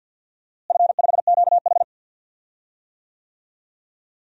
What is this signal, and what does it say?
Morse code, heard as a keyed tone, about 51 wpm, 700 Hz. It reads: F5QH
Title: DXCC-100 Call Signs - 4 Characters - Repeated 3x - 50wpm